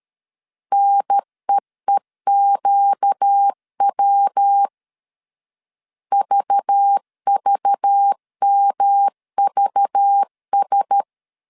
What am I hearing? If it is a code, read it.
Morse code, heard as a keyed tone, about 13 wpm, 800 Hz. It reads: NEEQW VVMVS
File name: CW / DET / 1 6 1500C S B 12wpm 3431